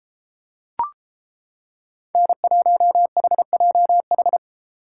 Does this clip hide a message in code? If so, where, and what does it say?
Yes: Morse code from 2.1 to 4.4 s, heard as a keyed tone, about 29 wpm, 700 Hz. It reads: N1HJH